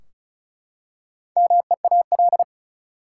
Morse code, heard as a keyed tone, about 36 wpm, 700 Hz. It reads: MEAL
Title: Four Letter Words - 35wpm